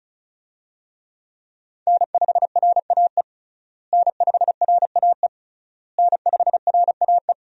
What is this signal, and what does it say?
Morse code, heard as a keyed tone, about 36 wpm, 700 Hz. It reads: N5RAE N5RAE N5RAE
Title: USA Call Signs - 5 Characters - Repeated 3x - 35wpm